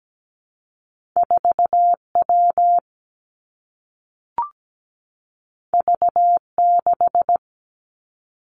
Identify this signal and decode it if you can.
Morse code, heard as a keyed tone, about 17 wpm, 700 Hz. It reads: 4W V6
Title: Two Challenging Letter-Number Combinations 17wpm